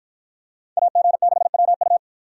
Morse code, heard as a keyed tone, about 53 wpm, 700 Hz. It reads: AG6WU